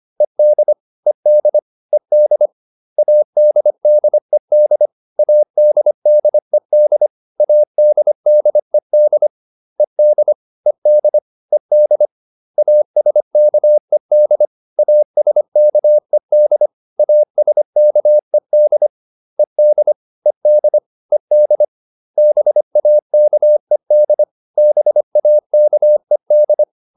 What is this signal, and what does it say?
Morse code, heard as a keyed tone, about 25 wpm, 600 Hz. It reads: ED ED ED ADDED ADDED ADDED ED ED ED ASKED ASKED ASKED ED ED ED BAKED BAKED